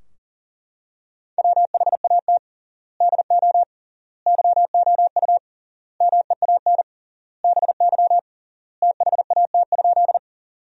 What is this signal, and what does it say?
Morse code, heard as a keyed tone, about 40 wpm, 700 Hz. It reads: WHAT DO YOU MEAN BY THAT?
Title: 3x Word-Spacing - Sentences from Top 200 Words - 40wpm